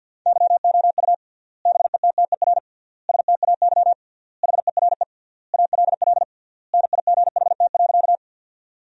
Morse code, heard as a keyed tone, about 47 wpm, 700 Hz. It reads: YOU BETTER STAY HERE ALL NIGHT.
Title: 3x Word-Spacing - Sentences from Top 500 Words - 50wpm